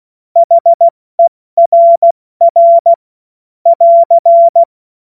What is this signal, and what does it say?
Morse code, heard as a keyed tone, about 16 wpm, 680 Hz. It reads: HERR <AR>
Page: cw-witze